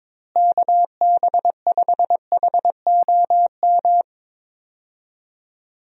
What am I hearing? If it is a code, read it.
Morse code, heard as a keyed tone, about 22 wpm, 700 Hz. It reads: KB5HOM
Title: USA Call Signs - 6 Characters - Repeated 3x - 22wpm